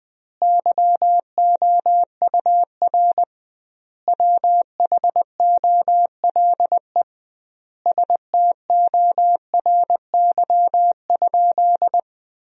Morse code, heard as a keyed tone, about 20 wpm, 700 Hz. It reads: YOUR WHOLE STORY?
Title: 2x Word-Spacing - Sentences from Top 400 Words - 20wpm